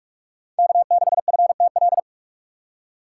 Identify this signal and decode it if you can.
Morse code, heard as a keyed tone, about 62 wpm, spaced as for 53 wpm, 700 Hz. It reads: K6FTL
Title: Speed Racing - USA 1x3 Call Signs 45wpm